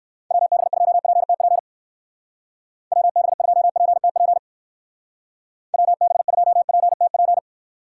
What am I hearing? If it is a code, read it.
Morse code, heard as a keyed tone, about 55 wpm, 700 Hz. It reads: WB2PTP WB2PTP WB2PTP
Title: Speed Racing - USA 2x3 Call Signs 45wpm